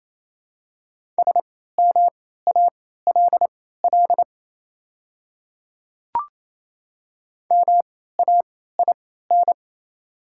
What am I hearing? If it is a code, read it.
Morse code, heard as a keyed tone, about 28 wpm, spaced as for 16 wpm, 690 Hz. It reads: SMALL MAIN
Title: 3x Character Spacing - Top 500 Words - 28wpm